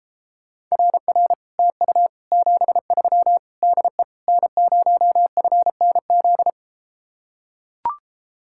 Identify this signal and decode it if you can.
Morse code, heard as a keyed tone, about 33 wpm, 700 Hz. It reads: RR TU 73 DE N0FNZ